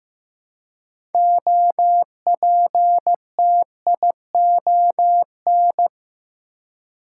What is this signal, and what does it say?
Morse code, heard as a keyed tone, about 15 wpm, 700 Hz. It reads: OPTION
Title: Six Letter Words - 15wpm